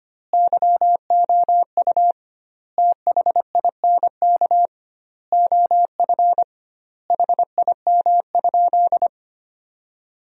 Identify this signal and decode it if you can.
Morse code, heard as a keyed tone, about 25 wpm, 700 Hz. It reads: YOU THINK OF HIM?
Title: 2x Word-Spacing - Sentences from Top 200 Words - 25wpm